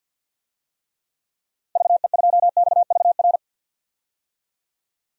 Morse code, heard as a keyed tone, about 53 wpm, 700 Hz. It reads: VE2XVR
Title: DXCC-100 Call Signs - 6 Characters - 50wpm